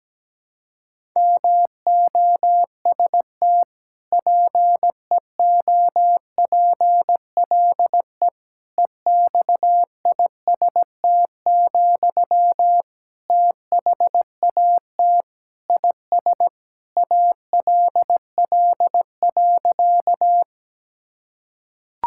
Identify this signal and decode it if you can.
Morse code, heard as a keyed tone, about 17 wpm, 700 Hz. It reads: MOST PEOPLE EXIST, THAT IS ALL.
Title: Sentences from Top 1000 Words - Review Part 5 - 17wpm